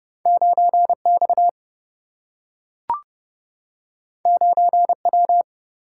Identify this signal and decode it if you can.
Morse code, heard as a keyed tone, about 30 wpm, 700 Hz. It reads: 9X 9W
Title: Single-Letter Single-Number 30wpm